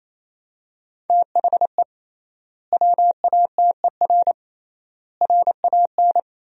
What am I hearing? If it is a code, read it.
Morse code, heard as a keyed tone, about 28 wpm, 700 Hz. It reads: THE WATER RAN
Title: 3x Word-Spacing - Sentences from Top 500 Words - 28wpm